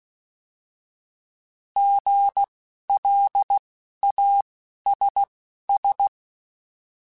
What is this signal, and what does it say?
Morse code, heard as a keyed tone, about 16 wpm, spaced as for 11 wpm, 800 Hz. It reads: GLASS
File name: Alphabetical / Alphabetical mp3s / Gs @10